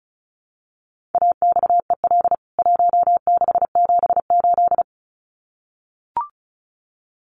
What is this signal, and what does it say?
Morse code, heard as a keyed tone, about 34 wpm, 700 Hz. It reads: AXEL 1678
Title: CWT Exchange 35wpm